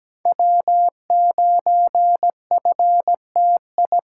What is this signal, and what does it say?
Morse code, heard as a keyed tone, about 17 wpm, 700 Hz. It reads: W9FTI
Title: USA Call Signs - 5 Characters - 4x Character Spacing - 17wpm